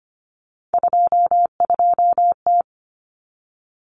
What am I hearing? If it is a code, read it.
Morse code, heard as a keyed tone, about 25 wpm, 700 Hz. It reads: 22T
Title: Common Cut Numbers - 3-Digit 25wpm